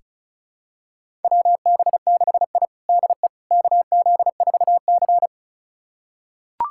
Morse code, heard as a keyed tone, about 36 wpm, 700 Hz. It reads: WB6I DE KZ4C
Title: USA Callsign DE Callsign - 4 Characters - 35wpm